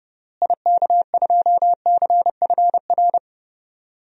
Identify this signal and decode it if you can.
Morse code, heard as a keyed tone, about 30 wpm, 700 Hz. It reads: IK2CFR
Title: DXCC-100 Call Signs - 6 Characters - Repeated 2x - 30wpm